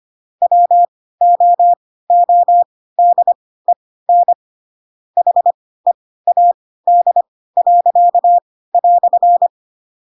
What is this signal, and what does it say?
Morse code, heard as a keyed tone, about 25 wpm, spaced as for 16 wpm, 700 Hz. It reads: WOODEN HEAD."
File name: Pinocchio / Pinocchio ch4 10 25 ik1hge